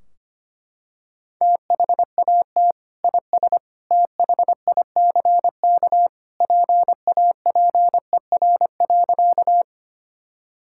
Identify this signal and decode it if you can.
Morse code, heard as a keyed tone, about 25 wpm, 700 Hz. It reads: THAT IS THICK PAPER.
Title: Sentences from Top 1600 Words 25wpm